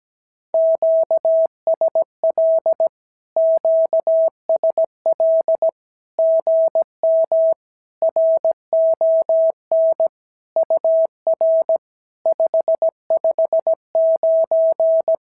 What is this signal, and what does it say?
Morse code, heard as a keyed tone, about 17 wpm, 650 Hz. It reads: QSL QSL GM RON UR 559
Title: SOTA QSO Practice 17wpm